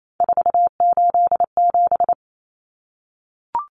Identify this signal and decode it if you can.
Morse code, heard as a keyed tone, about 28 wpm, 700 Hz. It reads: 487
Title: Numbers - 3-Digit 28wpm